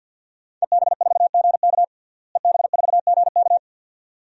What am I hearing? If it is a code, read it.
Morse code, heard as a keyed tone, about 53 wpm, 700 Hz. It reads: EB4GK EB4GK
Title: DXCC-100 Call Signs - 5 Characters - Repeated 2x - 50wpm